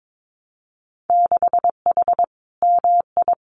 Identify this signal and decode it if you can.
Morse code, heard as a keyed tone, about 22 wpm, 700 Hz. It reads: 6H MI
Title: Winter Field Day Exchanges Encore 22wpm